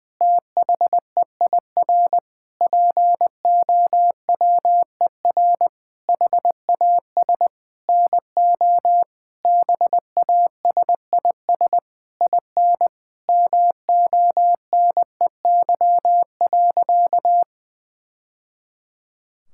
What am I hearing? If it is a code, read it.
Morse code, heard as a keyed tone, about 20 wpm, 700 Hz. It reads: THEIR POWER HAS NO BASIS IN MONEY.
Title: Sentences from Top 1000 Words - Review Part 5 - 20wpm